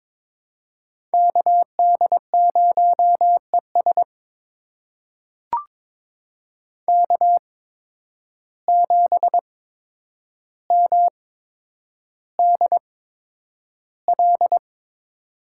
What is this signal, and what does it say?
Morse code, heard as a keyed tone, about 21 wpm, spaced as for 9 wpm, 700 Hz. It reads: KD0ES K7MDL
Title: USA Call Signs - 8x Character Spacing - 22wpm